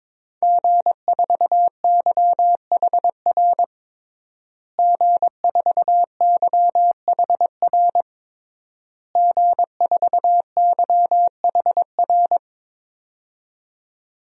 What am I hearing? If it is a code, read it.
Morse code, heard as a keyed tone, about 22 wpm, 700 Hz. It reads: G4YHR G4YHR G4YHR
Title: DXCC-100 Call Signs - 5 Characters - Repeated 3x - 22wpm